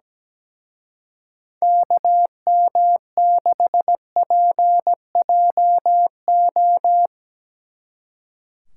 Morse code, heard as a keyed tone, about 18 wpm, 700 Hz. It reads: KM6PJO
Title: Speed Racing - USA Call Signs 17wpm